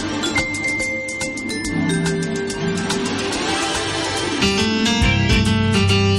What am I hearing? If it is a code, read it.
Dial-up modem handshake.
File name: wrct-log hi- 21